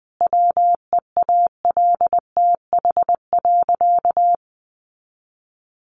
Morse code, heard as a keyed tone, about 20 wpm, 700 Hz. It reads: WEALTH.
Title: Sentences from Top 1400 Words 20wpm